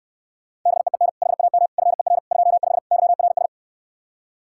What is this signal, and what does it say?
Morse code, heard as a keyed tone, about 68 wpm, 700 Hz. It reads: BEEN HAM FER 25 YRS